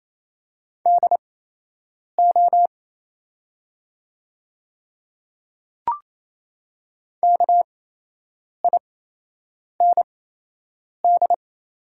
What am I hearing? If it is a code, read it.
Morse code, heard as a keyed tone, about 28 wpm, spaced as for 8 wpm, 700 Hz. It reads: DO KIND
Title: 8x Character Spacing - Top 200 Words - 28wpm